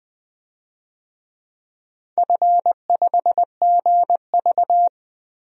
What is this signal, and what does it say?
Morse code, heard as a keyed tone, about 20 wpm, 700 Hz. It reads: F5GV
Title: DXCC-100 Call Signs - 4 Characters - 20wpm